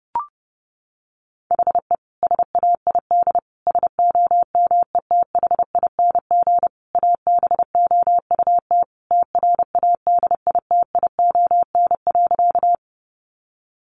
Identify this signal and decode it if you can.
Morse code, heard as a keyed tone, about 30 wpm, 700 Hz. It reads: HE SAID SOMETHING ABOUT TRADITION.
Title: Sentences from Top 1500 Words 30wpm